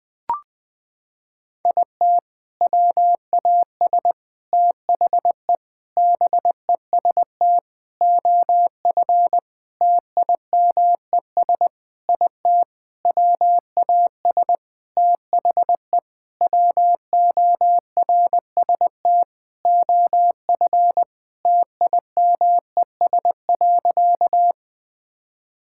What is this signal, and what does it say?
Morse code, heard as a keyed tone, about 20 wpm, 700 Hz. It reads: IT WAS THE BEST OF TIMES IT WAS THE WORST OF TIMES.
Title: Sentences with 8-Word Phrases - 20wpm